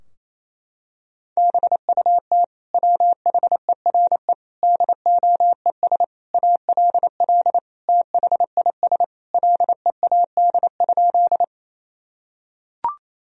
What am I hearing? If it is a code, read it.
Morse code, heard as a keyed tone, about 27 wpm, 700 Hz. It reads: BUT WHERE DOES ALL THIS LEAD?
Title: Sentences from Top 1000 Words - Review Part 2 - 28wpm